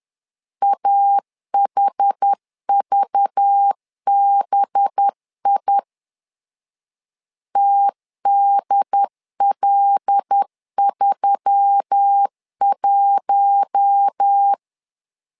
Morse code, heard as a keyed tone, about 10 wpm, 800 Hz. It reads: AHVBI TDL31
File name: LEZIONI / 5 LBVC / 5 LBV 10wpm 2248